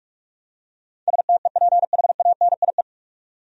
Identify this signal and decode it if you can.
Morse code, heard as a keyed tone, about 47 wpm, 700 Hz. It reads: STEPHANIE
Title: USA Common Names - 45wpm